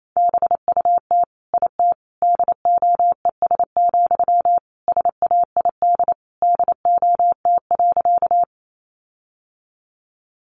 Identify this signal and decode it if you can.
Morse code, heard as a keyed tone, about 28 wpm, 700 Hz. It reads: BUT IT DOES, SAID DOT.
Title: Sentences from Top 1500 Words 28wpm